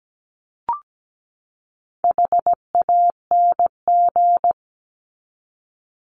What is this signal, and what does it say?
Morse code, heard as a keyed tone, about 17 wpm, 700 Hz. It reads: HANG
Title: Top 1501-1750 Words 17wpm